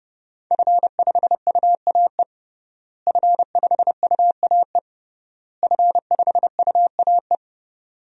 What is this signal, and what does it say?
Morse code, heard as a keyed tone, about 30 wpm, 700 Hz. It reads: F5UAE F5UAE F5UAE
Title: DXCC-100 Call Signs - 5 Characters - Repeated 3x - 30wpm